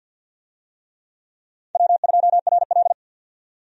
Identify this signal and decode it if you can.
Morse code, heard as a keyed tone, about 47 wpm, 700 Hz. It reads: W2RL
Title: USA Call Signs - 4 Characters - 50wpm